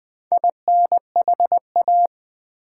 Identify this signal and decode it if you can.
Morse code, heard as a keyed tone, about 20 wpm, 700 Hz. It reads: INHA
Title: Build A Word - 4 Letters At A Time - 10-Letter Words - 20wpm